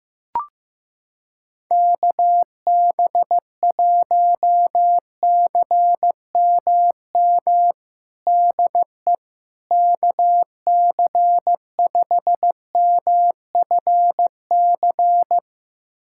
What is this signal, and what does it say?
Morse code, heard as a keyed tone, about 15 wpm, 700 Hz. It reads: KB1CMM DE KC5MFC
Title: USA Callsign DE Callsign - 6 Characters - 15wpm